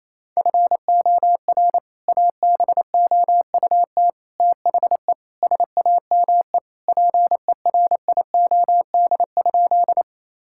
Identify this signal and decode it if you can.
Morse code, heard as a keyed tone, about 27 wpm, 700 Hz. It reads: FOR ABOUT THE SAME PERIOD?
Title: Sentences from Top 1000 Words - Review Part 5 - 28wpm